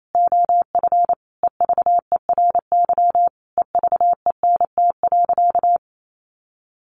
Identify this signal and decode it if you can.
Morse code, heard as a keyed tone, about 28 wpm, 700 Hz. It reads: OF EVERY EVENT.
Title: Sentences from Top 1400 Words 28wpm